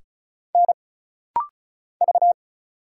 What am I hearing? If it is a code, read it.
Morse code, heard as a keyed tone, about 36 wpm, 700 Hz. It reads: N V
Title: Single Letters - Rapid-Fire 35wpm